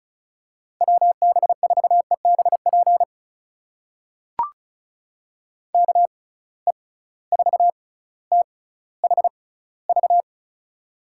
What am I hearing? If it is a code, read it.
Morse code, heard as a keyed tone, about 36 wpm, spaced as for 18 wpm, 700 Hz. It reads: WB4EBP KE4THV
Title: USA Call Signs - 6 Characters - 6x Character Spacing - 35wpm